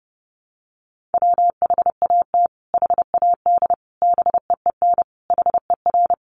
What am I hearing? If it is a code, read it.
Morse code, heard as a keyed tone, about 28 wpm, 700 Hz. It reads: WHAT HAD BEEN HER
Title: Sentences from Top 1500 Words 30wpm